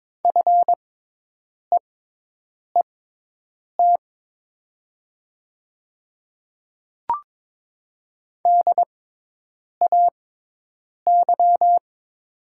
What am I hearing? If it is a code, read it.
Morse code, heard as a keyed tone, about 22 wpm, spaced as for 8 wpm, 700 Hz. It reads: FEET DAY
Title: 6x Character Spacing - Top 300 Words - 22wpm